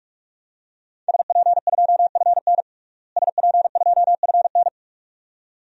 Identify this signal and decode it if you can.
Morse code, heard as a keyed tone, about 44 wpm, 700 Hz. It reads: SP2FN SP2FN
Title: DXCC-100 Call Signs - 5 Characters - Repeated 2x - 45wpm